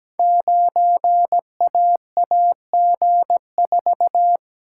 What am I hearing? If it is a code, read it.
Morse code, heard as a keyed tone, about 17 wpm, 700 Hz. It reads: 9AAG4
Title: Code Groups - 5 Letter:Numbers - 17wpm